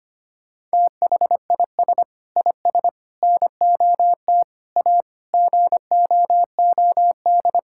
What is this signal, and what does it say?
Morse code, heard as a keyed tone, about 25 wpm, 700 Hz. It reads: THIS IS NOT A GOOD
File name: practice / 25wpm / Sentences with 4-Word Phrases - 25wpm